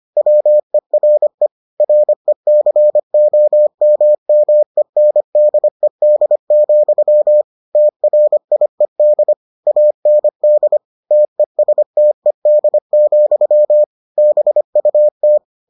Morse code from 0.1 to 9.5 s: WERE RECOMMENDED, TRIED

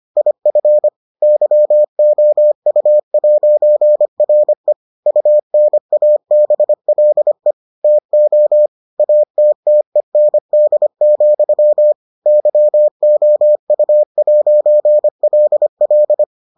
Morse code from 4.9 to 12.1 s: UNABLE TO ATTEND,

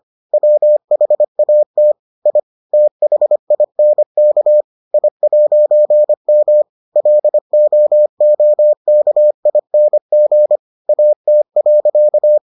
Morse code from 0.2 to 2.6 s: WHAT I

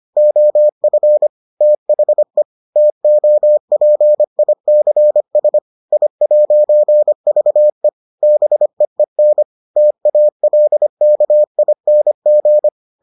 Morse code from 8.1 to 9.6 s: BEEN

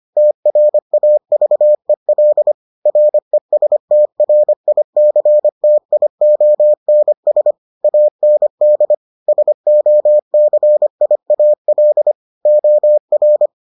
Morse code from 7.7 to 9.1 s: AND

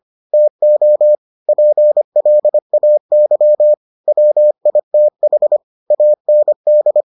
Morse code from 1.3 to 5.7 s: PLAY WITH

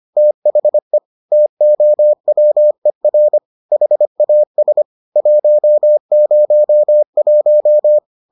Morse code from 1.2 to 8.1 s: TOWER HAS 101

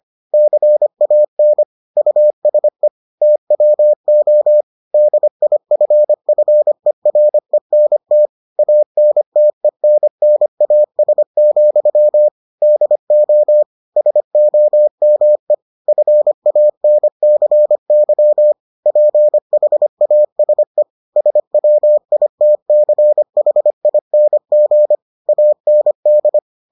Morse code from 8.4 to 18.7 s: ANTENNAS, DO SOME FANCY

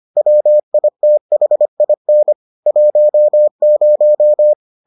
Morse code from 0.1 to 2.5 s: WITHIN

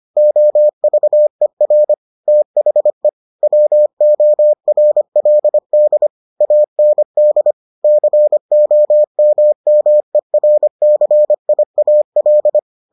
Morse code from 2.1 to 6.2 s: THE WORLD

